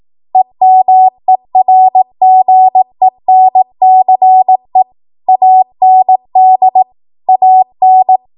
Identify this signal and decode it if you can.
Morse code, heard as a keyed tone, about 18 wpm, 750 Hz. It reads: EMERGENCE AND AN